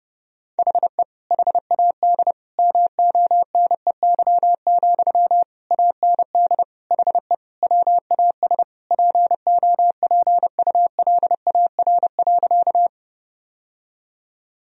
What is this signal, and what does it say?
Morse code, heard as a keyed tone, about 30 wpm, 700 Hz. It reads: HE HAD MONEY, AND HE WAS POPULAR.
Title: Sentences from Top 1000 Words - Review Part 3 - 30wpm